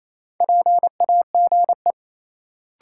Morse code, heard as a keyed tone, about 28 wpm, 700 Hz. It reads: PAGE
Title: Sentences from Top 1000 Words - Review Part 1 - New Word Drill - 28wpm